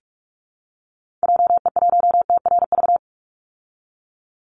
Morse code, heard as a keyed tone, about 44 wpm, 700 Hz. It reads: JE1TRV
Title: Call Signs Repeated 3x - 45wpm